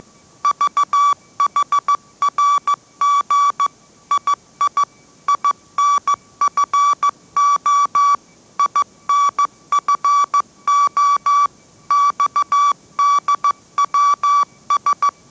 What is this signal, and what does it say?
Morse code, heard as a keyed tone, about 14 wpm, 1150 Hz. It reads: VHRG II INFO INFO XDWS
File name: mrx / mrx065 16wpm